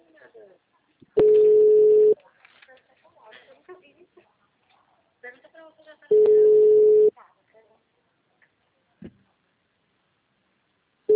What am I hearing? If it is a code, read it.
Ringback tone (Europe, 425 Hz), with speech.
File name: out- -2010- - - .59615